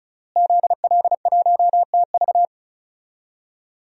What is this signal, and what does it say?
Morse code, heard as a keyed tone, about 36 wpm, 700 Hz. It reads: ZL1TV
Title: DXCC-100 Call Signs - 5 Characters - Repeated 3x - 35wpm